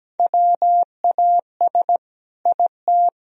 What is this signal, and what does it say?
Morse code, heard as a keyed tone, about 17 wpm, 700 Hz. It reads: WAS IT
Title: Sentences from Top 1000 Words - Review Part 4 - 17wpm